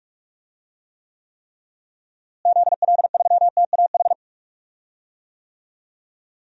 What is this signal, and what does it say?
Morse code, heard as a keyed tone, about 46 wpm, 700 Hz. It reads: ZL3TAH
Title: DXCC-100 Call Signs - 6 Characters - Repeated 3x - 45wpm